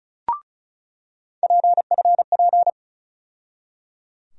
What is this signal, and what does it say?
Morse code, heard as a keyed tone, about 36 wpm, 700 Hz. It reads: PFP
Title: USA Call Sign Postfixes - 3 Letters 35wpm